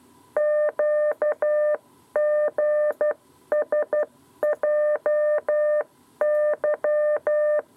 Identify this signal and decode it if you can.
Morse code, heard as a keyed tone, about 12 wpm, 600 Hz. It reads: QGSJY
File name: 7b lezione cw 2